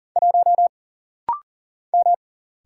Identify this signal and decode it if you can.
Morse code, heard as a keyed tone, about 40 wpm, 700 Hz. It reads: 1 M
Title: Single Letter-Number Alt - Rapid-Fire 40wpm